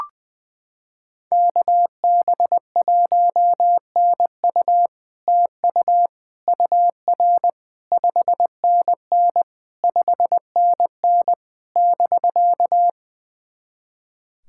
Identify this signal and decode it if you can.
Morse code, heard as a keyed tone, about 20 wpm, 700 Hz. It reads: KB1NU TU UR 5NN 5NN <BK>
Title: SOTA QSO Practice 20wpm